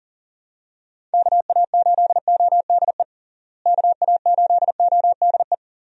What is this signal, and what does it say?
Morse code, heard as a keyed tone, about 40 wpm, 700 Hz. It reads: KA8ODE KA8ODE
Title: USA Call Signs - 6 Characters - Repeated 2x - 40wpm